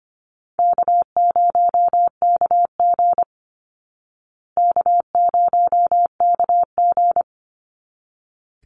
Morse code, heard as a keyed tone, about 25 wpm, 700 Hz. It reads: K0KG K0KG
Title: Easy Call Signs Repeated 2x - 25wpm